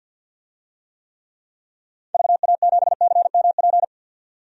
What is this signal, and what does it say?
Morse code, heard as a keyed tone, about 49 wpm, 700 Hz. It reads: VA7CMP
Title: DXCC-100 Call Signs - 6 Characters - 50wpm